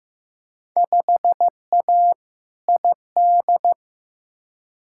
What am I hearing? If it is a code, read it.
Morse code, heard as a keyed tone, about 15 wpm, 700 Hz. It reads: 5A ID
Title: Field Day Exchanges Encore - 15wpm